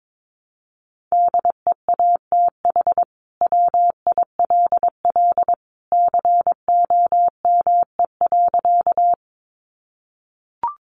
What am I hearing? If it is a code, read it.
Morse code, heard as a keyed tone, about 22 wpm, 700 Hz. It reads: DEATH WILL COME.
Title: Sentences from Top 1100 Words 22wpm